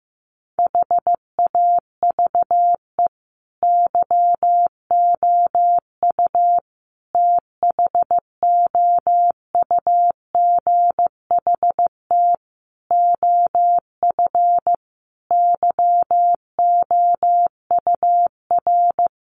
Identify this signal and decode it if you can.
Morse code, heard as a keyed tone, about 15 wpm, 700 Hz. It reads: HAVE YOU THOUGHT OF YOUR
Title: Sentences from Top 700 Words 15wpm